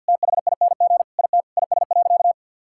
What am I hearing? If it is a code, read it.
Morse code, heard as a keyed tone, about 50 wpm, 700 Hz. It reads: THING IT IS.
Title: Sentences from Top 1000 Words - Review Part 6 - 50wpm